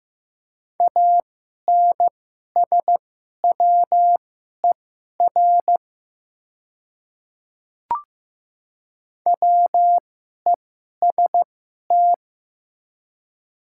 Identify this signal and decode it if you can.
Morse code, heard as a keyed tone, about 15 wpm, spaced as for 11 wpm, 690 Hz. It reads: ANSWER WEST
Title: 2x Character Spacing - Top 500 Words - 15wpm